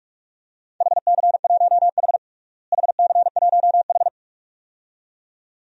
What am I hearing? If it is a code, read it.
Morse code, heard as a keyed tone, about 45 wpm, 700 Hz. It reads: HC1H HC1H
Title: DXCC-100 Call Signs - 4 Characters - Repeated 2x - 45wpm